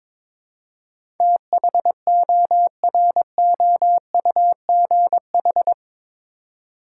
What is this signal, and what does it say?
Morse code, heard as a keyed tone, about 22 wpm, 700 Hz. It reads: THOROUGH
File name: practice / 22wpm / Eight Letter Words - 22wpm